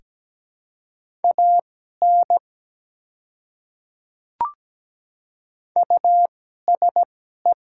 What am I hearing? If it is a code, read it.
Morse code, heard as a keyed tone, about 17 wpm, spaced as for 12 wpm, 700 Hz. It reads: AN USE
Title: 2x Character Spacing - Top 100 Words - 17wpm